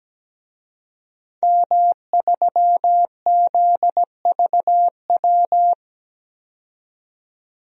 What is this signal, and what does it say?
Morse code, heard as a keyed tone, about 17 wpm, 700 Hz. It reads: M3ZVW
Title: DXCC-100 Call Signs - 5 Characters - 17wpm